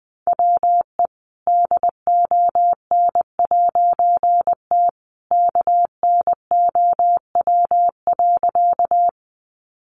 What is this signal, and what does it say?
Morse code, heard as a keyed tone, about 20 wpm, 700 Hz. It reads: WE DON'T KNOW.
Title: A Sound Of Thunder 20wpm